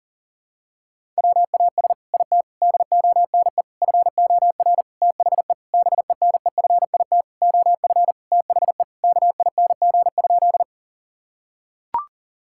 Morse code, heard as a keyed tone, about 40 wpm, 700 Hz. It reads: WAS IT DONE FOR THE BENEFIT OF THE KING?
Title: Sentences from Top 1000 Words - Review Part 6 - 40wpm